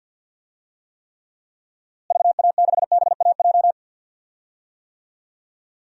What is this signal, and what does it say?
Morse code, heard as a keyed tone, about 51 wpm, 700 Hz. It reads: VA6BAJ
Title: DXCC-100 Call Signs - 6 Characters - Repeated 3x - 50wpm